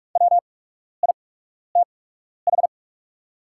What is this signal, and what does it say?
Morse code, heard as a keyed tone, about 44 wpm, spaced as for 12 wpm, 700 Hz. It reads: WITH
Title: 8x Character Spacing - Top 100 Words - 45wpm